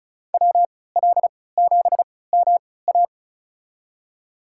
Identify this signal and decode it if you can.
Morse code, heard as a keyed tone, about 36 wpm, spaced as for 20 wpm, 700 Hz. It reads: WL7MA
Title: USA Call Signs - 3x Character Spacing - 35wpm